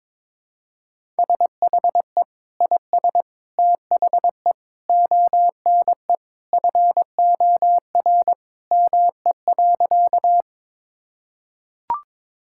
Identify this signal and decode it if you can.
Morse code, heard as a keyed tone, about 22 wpm, 700 Hz. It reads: SHE IS THE ONE FOR ME.
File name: practice / 22wpm / Sentences with 2-Word Phrases - 22wpm